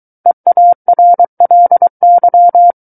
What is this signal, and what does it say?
Morse code, heard as a keyed tone, about 23 wpm, 700 Hz. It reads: EARLY
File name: Book One / 14-Chapter 14 - 23 WPM